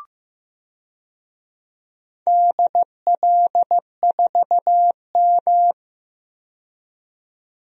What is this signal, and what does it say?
Morse code, heard as a keyed tone, about 15 wpm, 700 Hz. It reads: DL4M
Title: DXCC-100 Call Signs - 4 Characters - 15wpm